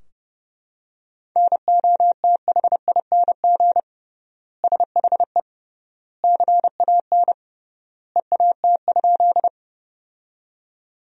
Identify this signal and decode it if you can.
Morse code, heard as a keyed tone, about 29 wpm, 700 Hz. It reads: NOTHING SHE CAN EAT?
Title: 3x Word-Spacing - Sentences from Top 500 Words - 30wpm